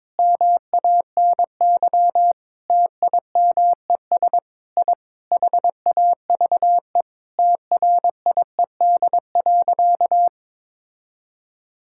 Morse code, heard as a keyed tone, about 22 wpm, 700 Hz. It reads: MANY TIMES I HAVE TRIED.
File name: Sentences with 7-Word Phrases - 22wpm